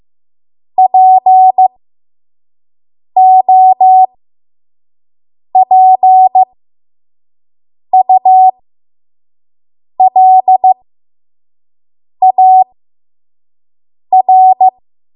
Morse code, heard as a keyed tone, about 15 wpm, spaced as for 5 wpm, 750 Hz. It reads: POPULAR
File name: Morse / 05WPM